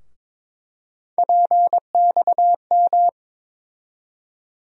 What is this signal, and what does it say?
Morse code, heard as a keyed tone, about 22 wpm, 700 Hz. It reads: PXM
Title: USA Call Sign Postfixes - 3 Letters 22wpm